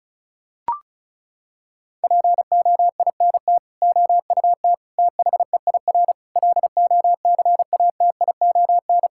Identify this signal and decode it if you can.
Morse code, heard as a keyed tone, about 34 wpm, 700 Hz. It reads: POINT OUT THEIR LOCATION
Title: Sentences from Top 1000 Words - Review Part 6 - 35wpm